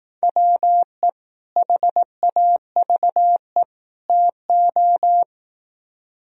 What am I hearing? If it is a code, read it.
Morse code, heard as a keyed tone, about 18 wpm, 700 Hz. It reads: WE HAVE TO
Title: Speed-Racing - Sets of 3 Words 15wpm